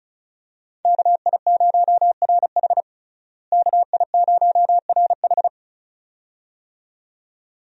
Morse code, heard as a keyed tone, about 34 wpm, 700 Hz. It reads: KI0RH KI0RH
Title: USA Call Signs - 5 Characters - Repeated 2x - 35wpm